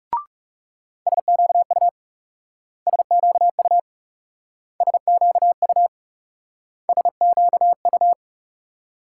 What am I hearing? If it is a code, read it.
Morse code, heard as a keyed tone, about 38 wpm, 700 Hz. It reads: SQU SQU SQU SQU